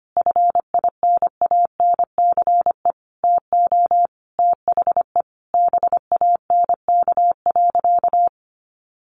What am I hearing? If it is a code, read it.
Morse code, heard as a keyed tone, about 25 wpm, 700 Hz. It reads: FINANCE TO THE BANK.
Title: Sentences from Top 1500 Words 25wpm